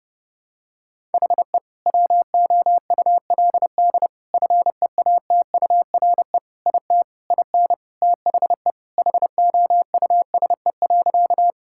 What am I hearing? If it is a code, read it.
Morse code, heard as a keyed tone, about 30 wpm, 700 Hz. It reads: HE WOULD FEATURE IT IN THE HOUSE.
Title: Sentences from Top 1000 Words - Review Part 5 - 30wpm